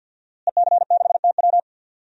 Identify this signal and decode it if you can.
Morse code, heard as a keyed tone, about 46 wpm, 700 Hz. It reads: EC6TW
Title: DXCC-100 Call Signs - 5 Characters - Repeated 3x - 50wpm